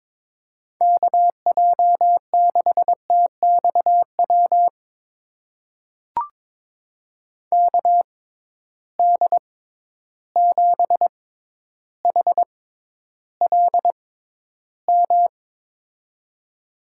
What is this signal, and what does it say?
Morse code, heard as a keyed tone, about 22 wpm, spaced as for 11 wpm, 700 Hz. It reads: KJ6TXW KD7HLM
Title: USA Call Signs - 6 Characters - 6x Character Spacing - 22wpm